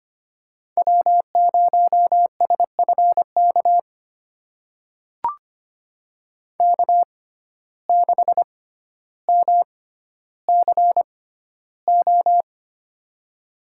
Morse code, heard as a keyed tone, about 25 wpm, spaced as for 13 wpm, 700 Hz. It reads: W0SFK K6MCO
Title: USA Call Signs - 5 Characters - 6x Character Spacing - 25wpm